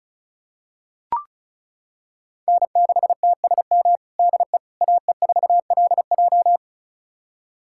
Morse code, heard as a keyed tone, about 34 wpm, 700 Hz. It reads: N6TSM DE AE4LJ
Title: USA Callsign DE Callsign - 5 Characters - 35wpm